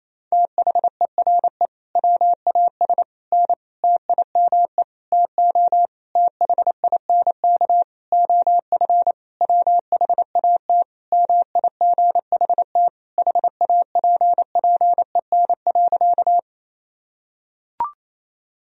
Morse code, heard as a keyed tone, about 28 wpm, 700 Hz. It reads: THERE WAS N TIME TO THINK OF WHAT MIGHT HAPPEN.
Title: Sentences from Top 400 Words and Abbreviations 28wpm